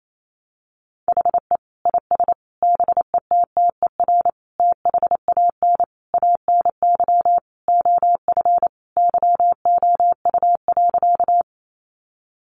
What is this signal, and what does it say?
Morse code, heard as a keyed tone, about 28 wpm, 700 Hz. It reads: HE IS BETTER THAN ANY OF YOU.
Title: Sentences from Top 400 Words 28wpm